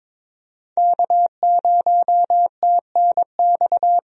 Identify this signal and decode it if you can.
Morse code, heard as a keyed tone, about 22 wpm, 700 Hz. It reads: K0TNX
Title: USA Call Signs - 5 Characters - Repeated 3x - 22wpm